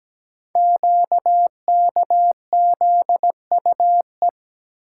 Morse code, heard as a keyed tone, about 17 wpm, 700 Hz. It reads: QKZUE
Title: Code Groups - 5 Letters - 17wpm